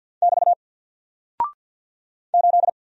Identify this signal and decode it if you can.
Morse code, heard as a keyed tone, about 48 wpm, 700 Hz. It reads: <BT> 8
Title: Single Character (Letter:Number:Punctuation:Prosign) Alt 50wpm